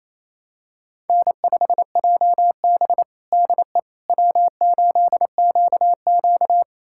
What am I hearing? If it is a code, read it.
Morse code, heard as a keyed tone, about 28 wpm, 700 Hz. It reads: N5JB DE W8QQ
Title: USA Callsign DE Callsign - 4 Characters - 28wpm